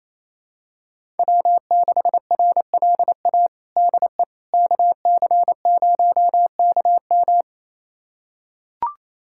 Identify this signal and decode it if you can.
Morse code, heard as a keyed tone, about 28 wpm, 700 Hz. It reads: W6RLA DE KC0KM
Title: USA Callsign DE Callsign - 5 Characters - 28wpm